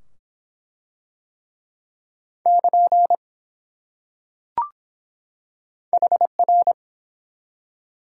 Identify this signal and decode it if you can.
Morse code, heard as a keyed tone, about 22 wpm, 700 Hz. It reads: <KN> HR